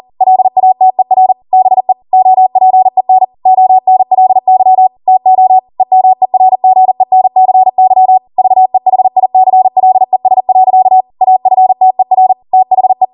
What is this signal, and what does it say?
Morse code, heard as a keyed tone, about 40 wpm, 750 Hz. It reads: LATER BE OPEN ONLY TO EMERGENCY VEHICLES. AFTER THE